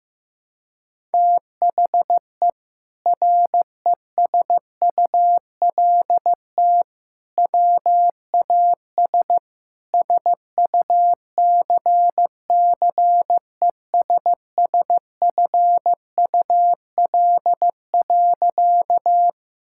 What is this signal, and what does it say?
Morse code, heard as a keyed tone, about 15 wpm, 700 Hz. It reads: THE RESULT WAS SUCCESSFUL.
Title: Sentences from Top 1000 Words - Review Part 6 - 15wpm